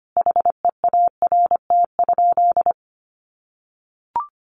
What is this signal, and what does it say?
Morse code, heard as a keyed tone, about 25 wpm, 700 Hz. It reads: HEART?
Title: Sentences from Top 1200 Words 25wpm